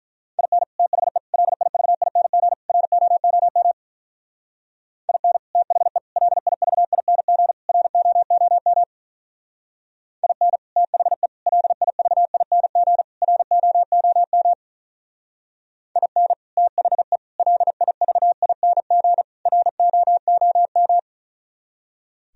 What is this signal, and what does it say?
Morse code, heard as a keyed tone, about 44 wpm, 700 Hz. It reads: IN THE LIVING ROOM IN THE LIVING ROOM IN THE LIVING ROOM IN THE LIVING ROOM